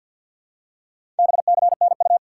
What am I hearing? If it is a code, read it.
Morse code, heard as a keyed tone, about 48 wpm, 700 Hz. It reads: BCNU
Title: Top 90 QSO Elements 50wpm